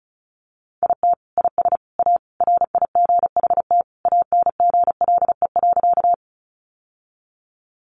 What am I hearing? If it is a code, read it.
Morse code, heard as a keyed tone, about 34 wpm, 700 Hz. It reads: IT IS A RIGHT ANGLE.
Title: Sentences from Top 1500 Words 35wpm